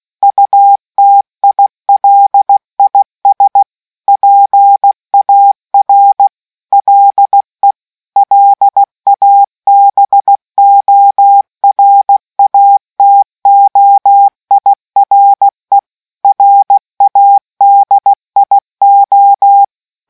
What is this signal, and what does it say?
Morse code, heard as a keyed tone, about 16 wpm, 800 Hz. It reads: UTILIS PAR LE LABORATOIRE RADIO